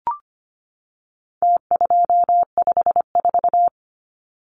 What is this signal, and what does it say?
Morse code, heard as a keyed tone, about 25 wpm, 700 Hz. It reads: T254
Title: Numbers - Contest Style - 4-Digit 25wpm